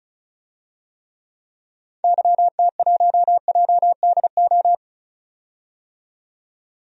Morse code, heard as a keyed tone, about 36 wpm, 690 Hz. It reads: YT1JDO
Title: DXCC-100 Call Signs - 6 Characters - 35wpm